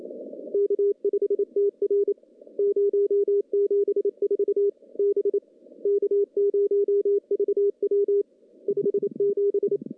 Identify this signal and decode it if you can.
Morse code, heard as a keyed tone, about 27 wpm, 410 Hz. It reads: K5TR 074 B K0VW 57
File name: contests / CW-SS-k5tr- -0200